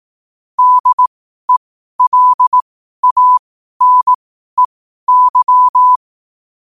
Morse code, heard as a keyed tone, about 18 wpm, spaced as for 12 wpm, 1000 Hz. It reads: DELANEY
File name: 12wpm